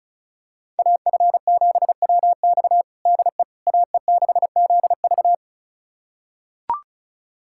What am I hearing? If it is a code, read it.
Morse code, heard as a keyed tone, about 36 wpm, 700 Hz. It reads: AF7WX DE AE6ZV
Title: USA Callsign DE Callsign - 5 Characters - 35wpm